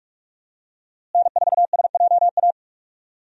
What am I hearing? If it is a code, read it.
Morse code, heard as a keyed tone, about 46 wpm, 700 Hz. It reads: N4SJA
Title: USA Call Signs - 5 Characters - Repeated 2x - 45wpm